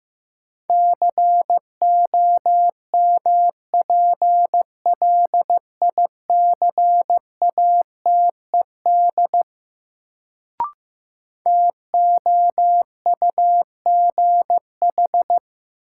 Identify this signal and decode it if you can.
Morse code, heard as a keyed tone, about 15 wpm, 700 Hz. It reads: COMPLICATED TOUGH
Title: Sentences from Top 1200 Words - New Word Drill - 15wpm